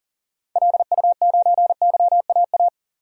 Morse code, heard as a keyed tone, about 40 wpm, 700 Hz. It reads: LU9YAA
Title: DXCC-100 Call Signs - 6 Characters - 40wpm